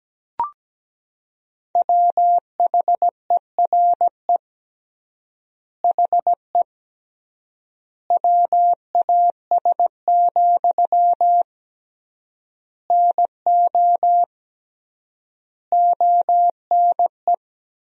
Morse code, heard as a keyed tone, about 17 wpm, 700 Hz. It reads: WHERE HE WAS, NO ONE
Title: 3x Word-Spacing - Sentences from Top 400 Words - 17wpm